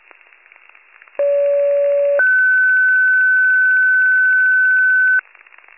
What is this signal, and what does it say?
Two-tone page, heard as a one-second tone, then a three-second tone, about 570 then 1500 Hz. It is heard from 1.2 to 5.2 s.